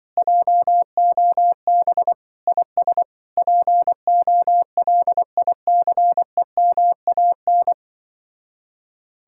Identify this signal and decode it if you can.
Morse code, heard as a keyed tone, about 24 wpm, 700 Hz. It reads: JOB IS POLICEMAN
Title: Speed-Racing Common QSO Phrases - 20wpm